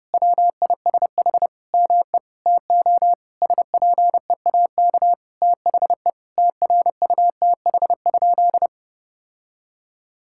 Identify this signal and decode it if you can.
Morse code, heard as a keyed tone, about 32 wpm, 700 Hz. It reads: WISH ME TO SPEAK THE TRUTH?
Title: Sentences from Top 1600 Words 30wpm